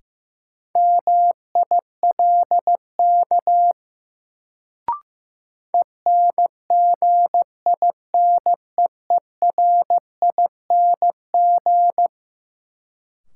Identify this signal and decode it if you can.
Morse code, heard as a keyed tone, about 15 wpm, 700 Hz. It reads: MILK ENGINEERING
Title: Sentences from Top 1500 Words - New Word Drill Alt - 15wpm